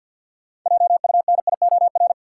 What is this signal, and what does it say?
Morse code, heard as a keyed tone, about 53 wpm, 700 Hz. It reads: JUNIOR
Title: Six Letter Words - 50wpm